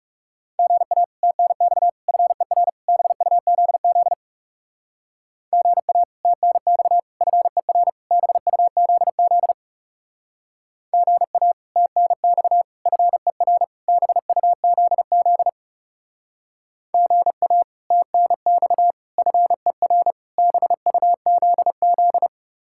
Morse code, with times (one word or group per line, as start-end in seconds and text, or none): 0.6-1.0 GA
1.2-1.9 TNX
2.1-2.7 FER
2.9-4.1 BUZZ
5.5-6.0 GA
6.2-7.0 TNX
7.2-7.9 FER
8.1-9.5 BUZZ
10.9-11.5 GA
11.8-12.6 TNX
12.9-13.6 FER
13.9-15.5 BUZZ
16.9-17.6 GA
17.9-18.9 TNX
19.2-20.1 FER
20.4-22.3 BUZZ